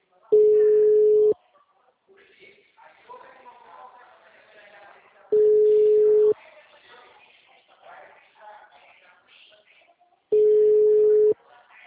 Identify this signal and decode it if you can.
Ringback tone (Europe, 425 Hz), with speech.